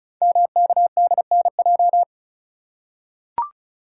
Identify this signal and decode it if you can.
Morse code, heard as a keyed tone, about 36 wpm, 700 Hz. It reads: MKDNJ
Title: Code Groups - 5 Letters - 35wpm